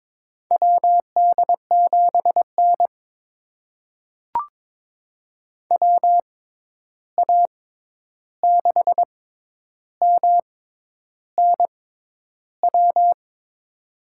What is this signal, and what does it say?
Morse code, heard as a keyed tone, about 21 wpm, spaced as for 10 wpm, 700 Hz. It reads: WD7N WA6MNW
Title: USA Call Signs - 6x Character Spacing - 22wpm